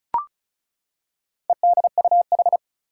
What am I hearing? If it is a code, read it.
Morse code, heard as a keyed tone, about 36 wpm, 700 Hz. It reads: EDUH
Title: Four Challenging Letter-Number Combinations 35wpm